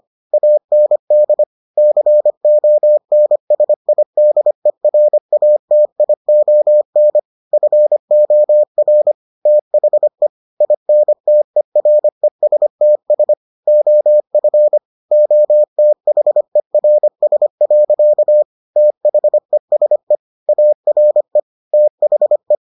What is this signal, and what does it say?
Morse code, heard as a keyed tone, about 25 wpm, 600 Hz. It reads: AND CONSIDERATION FOR THE INTERESTS OF OTHERS. THESE ARE THE